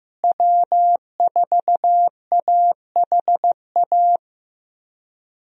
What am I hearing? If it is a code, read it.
Morse code, heard as a keyed tone, about 15 wpm, 700 Hz. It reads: W4AHA
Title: USA Call Signs - 5 Characters - Repeated 3x - 15wpm